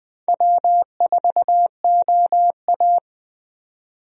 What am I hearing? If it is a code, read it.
Morse code, heard as a keyed tone, about 20 wpm, 700 Hz. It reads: W4OA
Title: USA Call Signs - 4 Characters - 20wpm